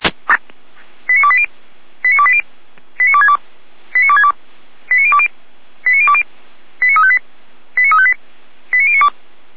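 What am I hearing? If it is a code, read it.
Five-tone selective call (ZVEI), heard as quick runs of five tones, tones of about 80 ms. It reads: 89280 89280 89262 89262 89020 89020 89358 89358 89092
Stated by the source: ZVEI five-tone selective call received on SDR, from the audio server of the Orlamünde fire brigade (feuerwehr-orlamuende).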